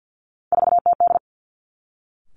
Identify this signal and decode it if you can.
Morse code, heard as a keyed tone, about 50 wpm, 700 Hz. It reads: 4TD
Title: Three Challenging Letter-Number Combinations 50wpm